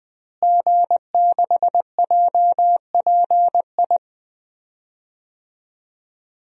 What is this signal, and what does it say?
Morse code, heard as a keyed tone, about 20 wpm, 700 Hz. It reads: G6JPI
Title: DXCC-100 Call Signs - 5 Characters - Repeated 3x - 20wpm